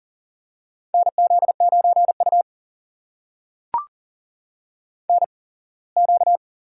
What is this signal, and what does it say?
Morse code, heard as a keyed tone, about 42 wpm, spaced as for 24 wpm, 700 Hz. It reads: NZ9U NQ
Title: USA Call Signs - 8x Character Spacing - 40wpm